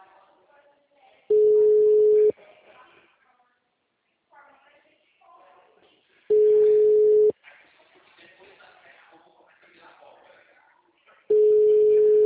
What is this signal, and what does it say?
Ringback tone (Europe, 425 Hz), with speech.
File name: out- -2014- - - .6010